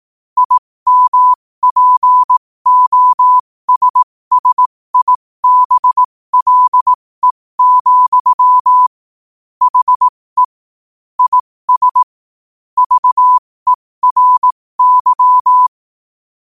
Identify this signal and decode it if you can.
Morse code, heard as a keyed tone, about 18 wpm, spaced as for 15 wpm, 1000 Hz. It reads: IMPOSSIBLE, HE IS VERY
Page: New quotes every day in morse code at 15 Words per minute.